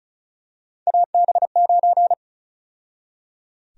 Morse code, heard as a keyed tone, about 34 wpm, 700 Hz. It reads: AB9
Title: USA Call Sign Prefixes - 35wpm